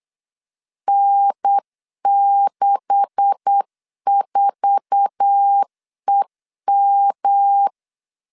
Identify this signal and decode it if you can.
Morse code, heard as a keyed tone, about 8 wpm, 800 Hz. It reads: N64EM
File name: LEZIONI / 9 286 8wpm 3425